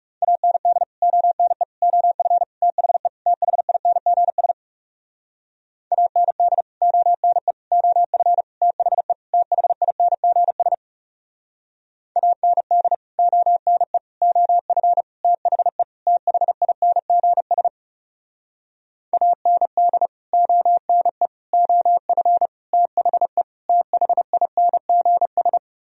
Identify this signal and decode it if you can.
Morse code, heard as a keyed tone, about 40 wpm, 700 Hz. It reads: AND ONE OF THE THINGS AND ONE OF THE THINGS AND ONE OF THE THINGS AND ONE OF THE THINGS